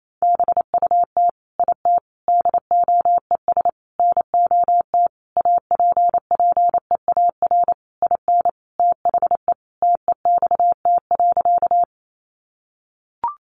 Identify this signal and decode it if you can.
Morse code, heard as a keyed tone, about 28 wpm, 700 Hz. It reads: BUT IT DOES NOT APPEAR IN THE TEXT.
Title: Sentences from Top 1200 Words 28wpm